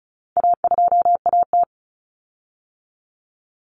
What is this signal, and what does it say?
Morse code, heard as a keyed tone, about 34 wpm, 700 Hz. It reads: A2AT
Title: Common Cut Numbers - 4-Digit 35wpm